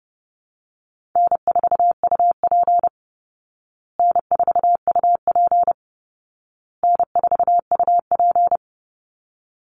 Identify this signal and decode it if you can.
Morse code, heard as a keyed tone, about 30 wpm, 700 Hz. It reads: N4UP N4UP N4UP
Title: Call Signs Repeated 2x - 30wpm